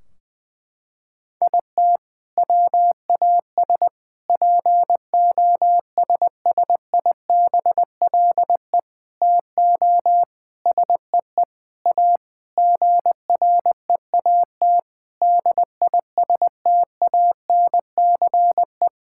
Morse code, heard as a keyed tone, about 20 wpm, 700 Hz. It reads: IT WAS POSSIBLE TO SEE A GREAT DISTANCE
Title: Sentences from Top 1600 Words 20wpm